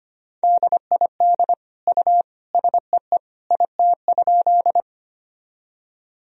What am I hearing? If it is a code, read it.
Morse code, heard as a keyed tone, about 25 wpm, 700 Hz. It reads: DID U SEE IT?
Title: Sentences from Top 100 Words and Abbreviations 25wpm